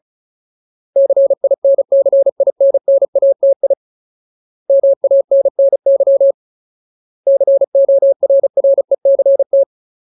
Morse code, heard as a keyed tone, about 34 wpm, 550 Hz. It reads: CINCINNATI MANNY CORRECT
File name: Ultimate Word List / 35WPM Ultimate